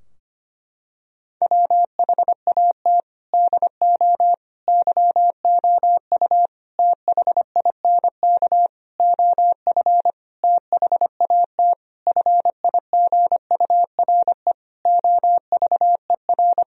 Morse code, heard as a keyed tone, about 25 wpm, 700 Hz. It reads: WHAT DO YOU THINK OF THAT FIGURE OVER
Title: Sentences from Top 1000 Words - Review Part 2 - 25wpm